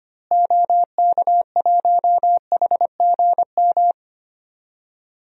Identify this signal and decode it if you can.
Morse code, heard as a keyed tone, about 25 wpm, 700 Hz. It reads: OK1HGM
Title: DXCC-100 Call Signs - 6 Characters - Repeated 2x - 25wpm